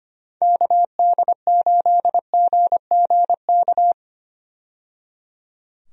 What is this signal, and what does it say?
Morse code, heard as a keyed tone, about 25 wpm, 700 Hz. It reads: KD8GGK
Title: USA Call Signs - 6 Characters - 25wpm